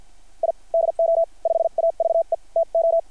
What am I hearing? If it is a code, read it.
Morse code, heard as a keyed tone, about 53 wpm, spaced as for 44 wpm, 690 Hz. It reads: I DO 5AVE TO
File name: archived / CRSnet